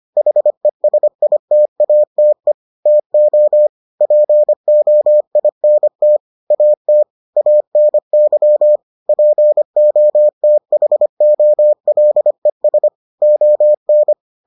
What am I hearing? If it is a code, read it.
Morse code, heard as a keyed tone, about 25 wpm, 600 Hz. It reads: HESITATE TO POINT AT ANY POTHOLES ON